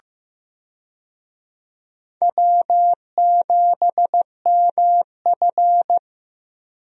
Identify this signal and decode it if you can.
Morse code, heard as a keyed tone, about 15 wpm, 700 Hz. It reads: W7MF
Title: USA Call Signs - 4 Characters - Repeated 2x - 15wpm